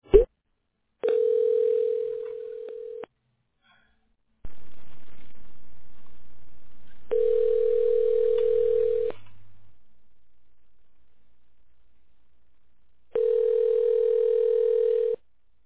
Ringback tone (North America), 440 + 480 Hz.